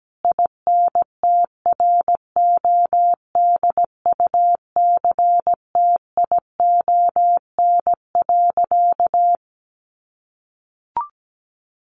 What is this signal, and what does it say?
Morse code, heard as a keyed tone, about 17 wpm, 700 Hz. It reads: INTRODUCTION.
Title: Sentences from Top 1000 Words 17wpm